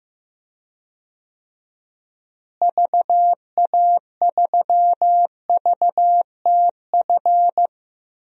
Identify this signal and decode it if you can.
Morse code, heard as a keyed tone, about 15 wpm, 700 Hz. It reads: VA3VTF
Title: DXCC-100 Call Signs - 6 Characters - Repeated 2x - 15wpm